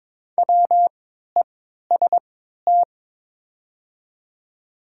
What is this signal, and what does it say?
Morse code, heard as a keyed tone, about 22 wpm, spaced as for 12 wpm, 700 Hz. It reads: WEST
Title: 3x Character Spacing - Top 500 Words - 22wpm